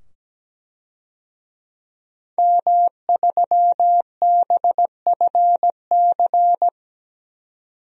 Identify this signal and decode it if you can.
Morse code, heard as a keyed tone, about 17 wpm, 700 Hz. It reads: M3BFC
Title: DXCC-100 Call Signs - 5 Characters - 17wpm